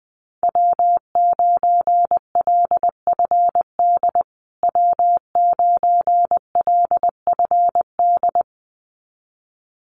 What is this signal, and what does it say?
Morse code, heard as a keyed tone, about 20 wpm, 700 Hz. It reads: W9LFD W9LFD
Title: USA Call Signs Repeated 2x 20wpm